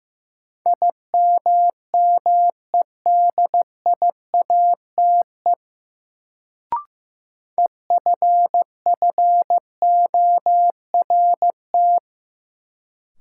Morse code, heard as a keyed tone, about 15 wpm, 700 Hz. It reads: IMMEDIATE EFFORT
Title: Sentences from Top 1200 Words - New Word Drill - 15wpm